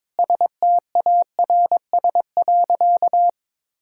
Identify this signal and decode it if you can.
Morse code, heard as a keyed tone, about 22 wpm, 700 Hz. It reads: STARS.
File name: practice / 22wpm / Sentences with 4-Word Phrases - 22wpm